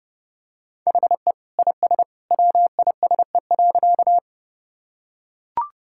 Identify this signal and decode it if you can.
Morse code, heard as a keyed tone, about 30 wpm, 700 Hz. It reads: HE IS WISE.
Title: Sentences from Top 1600 Words 30wpm